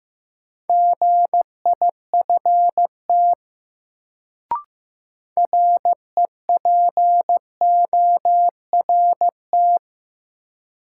Morse code, heard as a keyed tone, about 15 wpm, 700 Hz. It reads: GIFT REPORT
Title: Sentences from Top 900 Words - New Word Drill - 15wpm